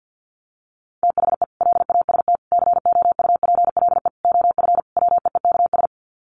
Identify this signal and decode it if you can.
Morse code, heard as a keyed tone, about 50 wpm, 700 Hz. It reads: THE LAST COUPLE OF WEEKS
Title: Sets of 5 Words Encore2 50wpm